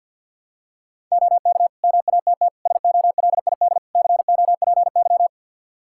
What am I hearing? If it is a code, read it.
Morse code, heard as a keyed tone, about 52 wpm, 700 Hz. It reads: OK MATT SOLID COPY
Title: Practice QSO - Exchange 3 - More Info - 50wpm